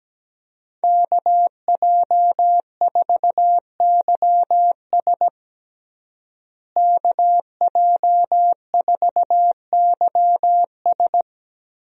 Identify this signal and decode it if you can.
Morse code, heard as a keyed tone, about 17 wpm, 700 Hz. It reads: KJ4YS KJ4YS
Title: USA Call Signs - 5 Characters - Repeated 3x - 17wpm